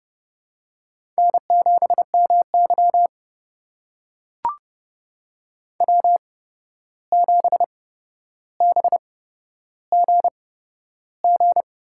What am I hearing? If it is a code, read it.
Morse code, heard as a keyed tone, about 30 wpm, spaced as for 12 wpm, 700 Hz. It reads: N7MY W7BGG
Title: USA Call Signs - 8x Character Spacing - 30wpm